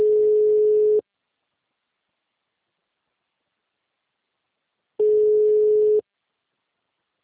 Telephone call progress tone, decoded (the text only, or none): ringback tone